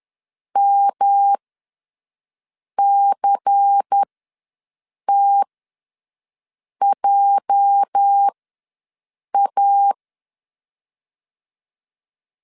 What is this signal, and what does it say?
Morse code, heard as a keyed tone, about 10 wpm, spaced as for 5 wpm, 800 Hz. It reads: MCTJA
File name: LEZIONI / 9 2 ws 10wpm 4130